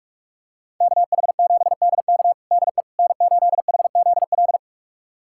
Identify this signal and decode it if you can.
Morse code, heard as a keyed tone, about 46 wpm, 700 Hz. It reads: KH7DK DE N8HZL
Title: USA Callsign DE Callsign - 5 Characters - 45wpm